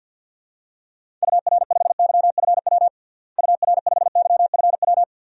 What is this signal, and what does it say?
Morse code, heard as a keyed tone, about 53 wpm, 700 Hz. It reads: UR5YFW UR5YFW
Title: DXCC-100 Call Signs - 6 Characters - Repeated 2x - 50wpm